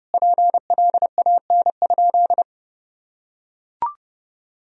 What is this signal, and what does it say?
Morse code, heard as a keyed tone, about 30 wpm, 700 Hz. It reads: PLAN?
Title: Sentences from Top 500 Words and Abbreviations 30wpm